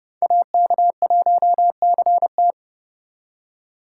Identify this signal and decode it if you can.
Morse code, heard as a keyed tone, about 37 wpm, 700 Hz. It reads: AK1CT
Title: Speed Racing - USA Call Signs 30wpm